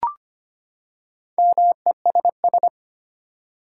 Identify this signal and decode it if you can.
Morse code, heard as a keyed tone, about 25 wpm, 700 Hz. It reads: MESS
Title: Four Letter Words - 25wpm